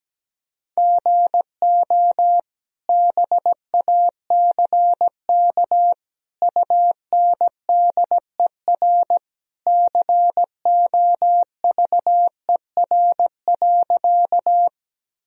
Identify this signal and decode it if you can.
Morse code, heard as a keyed tone, about 17 wpm, 700 Hz. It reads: GO BACK UNDER COVER.
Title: Sentences from Top 1000 Words - Review Part 1 - 17wpm